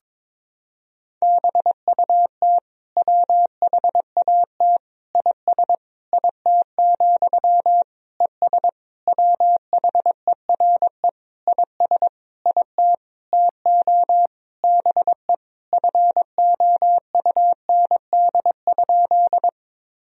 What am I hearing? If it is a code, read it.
Morse code, heard as a keyed tone, about 22 wpm, 700 Hz. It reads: BUT WHAT IS IT, ES WHERE IS IT TO BE FOUND?
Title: Sentences from Top 300 Words and Abbreviations 22wpm